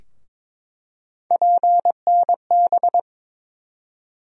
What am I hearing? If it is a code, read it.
Morse code, heard as a keyed tone, about 21 wpm, 700 Hz. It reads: PNB
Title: USA Call Sign Postfixes - 3 Letters 22wpm